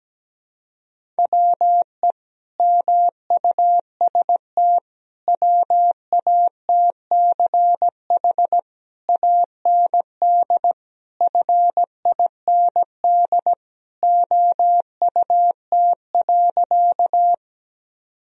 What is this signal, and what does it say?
Morse code, heard as a keyed tone, about 17 wpm, 700 Hz. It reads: WE MUST WATCH AND FIND OUT.
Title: Sentences from Top 1000 Words - Review Part 1 - 17wpm